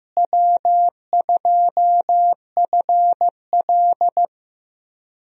Morse code, heard as a keyed tone, about 15 wpm, 700 Hz. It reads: W2FL
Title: USA Call Signs - 4 Characters - Repeated 2x - 15wpm